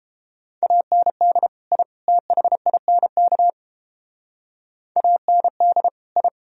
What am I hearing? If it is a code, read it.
Morse code, heard as a keyed tone, about 30 wpm, 700 Hz. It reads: AND I THINK AND I